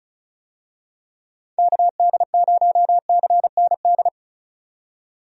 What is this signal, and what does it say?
Morse code, heard as a keyed tone, about 36 wpm, 700 Hz. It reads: KD0CND
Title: USA Call Signs - 6 Characters - 35wpm